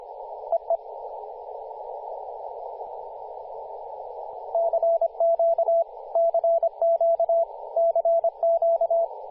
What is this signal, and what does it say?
Morse code, heard as a keyed tone, about 25 wpm, 670 Hz, with radio noise. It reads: EE CQ CQ CQ
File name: posts / 2016-july-07-2008z-v63gg-10-116mhz-cw